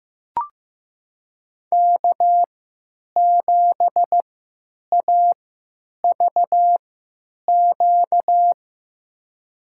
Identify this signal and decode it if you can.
Morse code, heard as a keyed tone, about 16 wpm, spaced as for 9 wpm, 700 Hz. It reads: K7AVQ
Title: USA Call Signs - 5 Characters - 3x Character Spacing - 15wpm